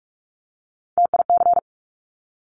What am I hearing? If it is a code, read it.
Morse code, heard as a keyed tone, about 46 wpm, 700 Hz. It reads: TIC
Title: Top Three Letter Combinations in Words 45wpm